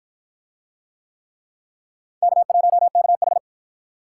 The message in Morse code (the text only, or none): K1KH